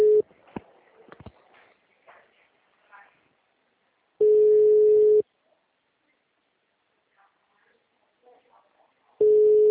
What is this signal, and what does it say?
Ringback tone (Europe, 425 Hz), with speech.